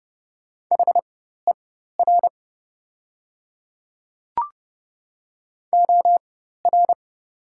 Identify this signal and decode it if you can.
Morse code, heard as a keyed tone, about 30 wpm, spaced as for 14 wpm, 700 Hz. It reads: HER OR
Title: 4x Character Spacing - Top 300 Words - 30wpm